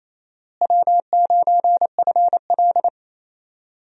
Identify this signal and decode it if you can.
Morse code, heard as a keyed tone, about 28 wpm, 700 Hz. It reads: W9FL
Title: USA Call Signs - 4 Characters - Repeated 2x - 28wpm